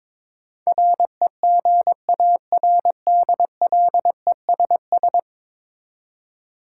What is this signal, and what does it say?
Morse code, heard as a keyed tone, about 22 wpm, 700 Hz. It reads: REGARDLESS
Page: morse-code-ninja-practice-22wpm